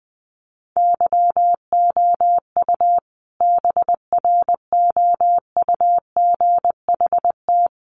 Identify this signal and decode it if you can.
Morse code, heard as a keyed tone, about 20 wpm, 700 Hz. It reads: YOU BROUGHT
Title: Sentences from Top 1300 Words 20wpm